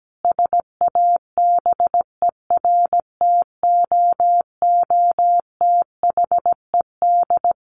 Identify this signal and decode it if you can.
Morse code, heard as a keyed tone, about 17 wpm, 700 Hz. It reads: SABERTOOTHED
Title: A Sound Of Thunder 17wpm